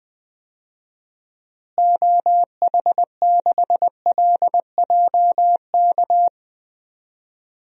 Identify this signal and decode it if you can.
Morse code, heard as a keyed tone, about 20 wpm, 700 Hz. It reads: OH6LJK
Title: DXCC-100 Call Signs - 6 Characters - Repeated 2x - 20wpm